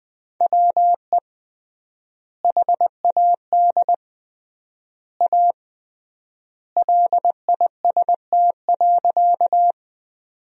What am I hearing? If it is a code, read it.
Morse code, heard as a keyed tone, about 20 wpm, 700 Hz. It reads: WE HAD A LIST.
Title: 3x Word-Spacing - Sentences from Top 400 Words - 20wpm